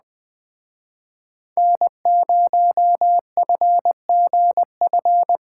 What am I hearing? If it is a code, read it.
Morse code, heard as a keyed tone, about 21 wpm, 700 Hz. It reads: N0FGF